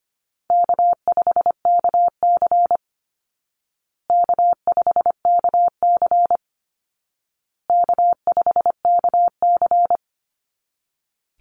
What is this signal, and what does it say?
Morse code, heard as a keyed tone, about 25 wpm, 700 Hz. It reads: K5KC K5KC K5KC
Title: Easy Call Signs Repeated 3x - 25wpm